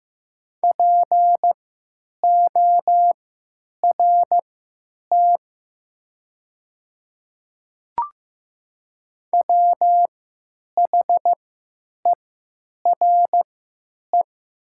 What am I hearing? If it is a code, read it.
Morse code, heard as a keyed tone, about 15 wpm, spaced as for 8 wpm, 700 Hz. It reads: PORT WHERE
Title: 3x Character Spacing - Top 200 Words - 15wpm